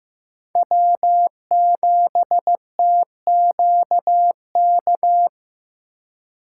Morse code, heard as a keyed tone, about 18 wpm, 700 Hz. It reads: W7TQK
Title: Speed Racing - USA 1x3 Call Signs 15wpm